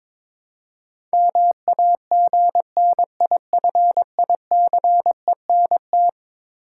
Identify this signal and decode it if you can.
Morse code, heard as a keyed tone, about 22 wpm, 700 Hz. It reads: MAGNIFICENT
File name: practice / 22wpm / Eleven Letter Words - 22wpm